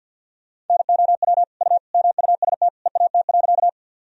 Morse code, heard as a keyed tone, about 46 wpm, 700 Hz. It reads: NOW U MUST EAT.
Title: Sentences from Top 300 Words and Abbreviations 50wpm